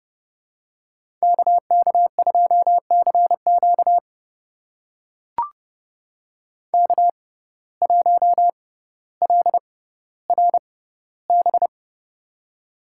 Morse code, heard as a keyed tone, about 31 wpm, spaced as for 16 wpm, 700 Hz. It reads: KK2CQ K1LRB
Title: USA Call Signs - 5 Characters - 6x Character Spacing - 30wpm